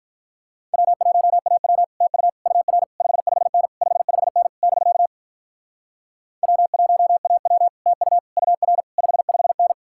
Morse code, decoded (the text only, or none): W1AW TU UR 55N 55N <BK> W1AW TU UR 55N